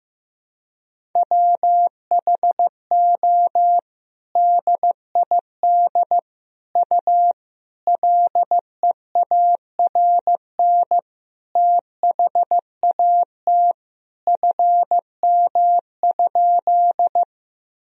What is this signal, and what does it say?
Morse code, heard as a keyed tone, about 16 wpm, 700 Hz. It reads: WHO DID U LEARN THAT FM?
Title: Sentences from Top 300 Words and Abbreviations 15wpm